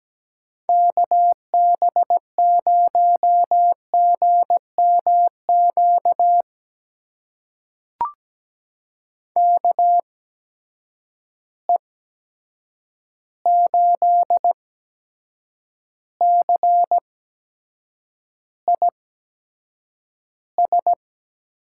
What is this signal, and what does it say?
Morse code, heard as a keyed tone, about 17 wpm, spaced as for 7 wpm, 700 Hz. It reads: KB0GMQ KE8CIS
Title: USA Call Signs - 6 Characters - 8x Character Spacing - 17wpm